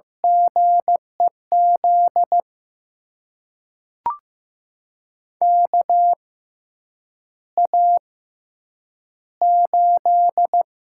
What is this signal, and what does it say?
Morse code, heard as a keyed tone, about 15 wpm, spaced as for 8 wpm, 700 Hz. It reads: GEZ KA8
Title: USA Call Signs - 6 Characters - 6x Character Spacing - 15wpm